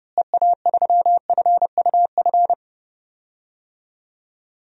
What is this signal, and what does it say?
Morse code, heard as a keyed tone, about 30 wpm, 700 Hz. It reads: EA3FUF
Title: DXCC-100 Call Signs - 6 Characters - Repeated 3x - 30wpm